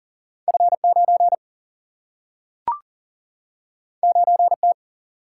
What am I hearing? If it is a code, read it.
Morse code, heard as a keyed tone, about 40 wpm, 700 Hz. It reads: F9 9T
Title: Single-Letter Single-Number 40wpm